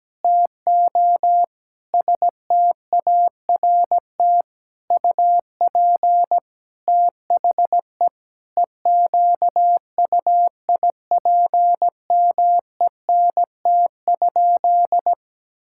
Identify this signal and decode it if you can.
Morse code, heard as a keyed tone, about 17 wpm, 700 Hz. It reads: TO START UP THE EQUIPMENT?
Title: Sentences from Top 1000 Words - Review Part 6 - 17wpm